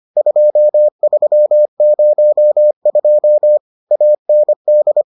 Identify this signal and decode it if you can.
Morse code, heard as a keyed tone, about 25 wpm, 600 Hz. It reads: 2302 AND